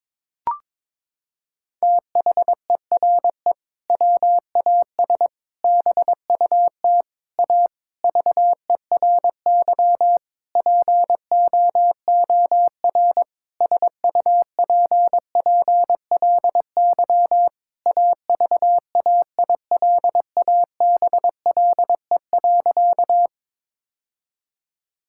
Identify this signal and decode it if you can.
Morse code, heard as a keyed tone, about 22 wpm, 700 Hz. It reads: THERE WAS BUT A VERY POOR SUPPLY AVAILABLE.
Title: Sentences from Top 1000 Words - Review Part 5 - 22wpm